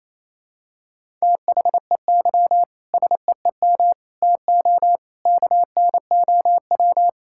Morse code, heard as a keyed tone, about 27 wpm, 700 Hz. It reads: THEY SEEM TO KNOW
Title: Sentences from Top 1000 Words - Review Part 6 - 28wpm